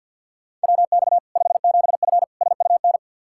Morse code, heard as a keyed tone, about 50 wpm, 700 Hz. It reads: WX 57F SUN
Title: K1USN SST QSO Practice 50wpm